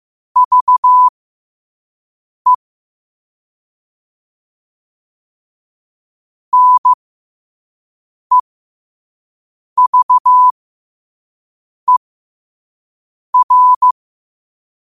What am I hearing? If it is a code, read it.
Morse code, heard as a keyed tone, about 15 wpm, spaced as for 5 wpm, 1000 Hz. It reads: VE NEVER
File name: qotdfiles / 5wpm